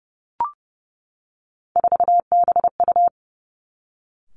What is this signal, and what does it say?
Morse code, heard as a keyed tone, about 30 wpm, 700 Hz. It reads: 4BU
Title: Three Challenging Letter-Number Combinations 30wpm